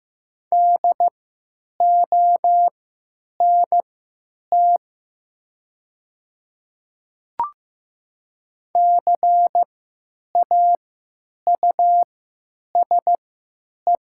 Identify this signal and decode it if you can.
Morse code, heard as a keyed tone, about 15 wpm, spaced as for 9 wpm, 700 Hz. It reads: DONT CAUSE
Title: 3x Character Spacing - Top 300 Words - 15wpm